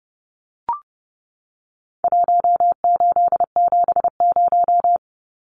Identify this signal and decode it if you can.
Morse code, heard as a keyed tone, about 30 wpm, 700 Hz. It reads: 1870
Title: Numbers - Contest Style - 4-Digit 30wpm